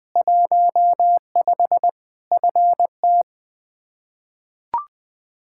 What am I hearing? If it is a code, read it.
Morse code, heard as a keyed tone, about 20 wpm, 700 Hz. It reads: 15 FT
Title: K1USN SST QSO Practice 20wpm